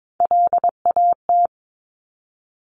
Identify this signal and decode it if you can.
Morse code, heard as a keyed tone, about 22 wpm, 700 Hz. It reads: LAT
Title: Top Three Letter Combinations in Words 22wpm